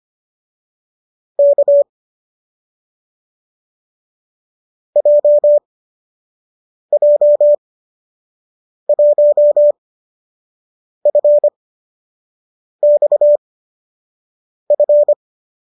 Morse code, heard as a keyed tone, about 25 wpm, spaced as for 6 wpm, 590 Hz. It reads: K JJ1FXF